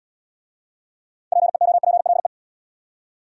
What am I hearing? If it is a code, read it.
Morse code, heard as a keyed tone, about 66 wpm, 690 Hz. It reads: PEOPLE